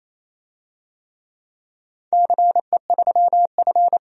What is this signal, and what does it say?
Morse code, heard as a keyed tone, about 28 wpm, 700 Hz. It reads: CE3F
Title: DXCC-100 Call Signs - 4 Characters - 28wpm